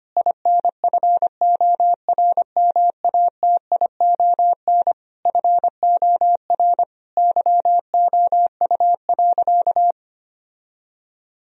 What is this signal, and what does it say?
Morse code, heard as a keyed tone, about 25 wpm, 700 Hz. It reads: INFORMATION FOR YOU.
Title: Sentences from Top 1000 Words - Review Part 3 - 25wpm